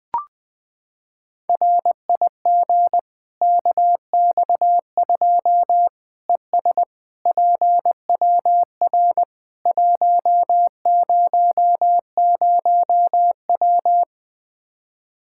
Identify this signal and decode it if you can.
Morse code, heard as a keyed tone, about 20 wpm, 700 Hz. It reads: RIG KX2 ES PWR 100W
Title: Practice QSO - Exchange 2 - Info - 20wpm